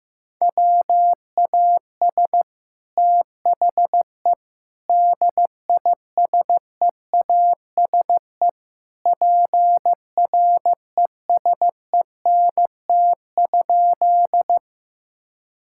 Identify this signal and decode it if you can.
Morse code, heard as a keyed tone, about 15 wpm, 700 Hz. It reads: WAS THE DISEASE PRESENT?
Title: Sentences from Top 1000 Words - Review Part 6 - 15wpm